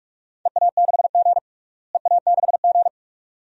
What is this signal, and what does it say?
Morse code, heard as a keyed tone, about 46 wpm, 700 Hz. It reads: EA6G EA6G
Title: DXCC-100 Call Signs - 4 Characters - Repeated 2x - 45wpm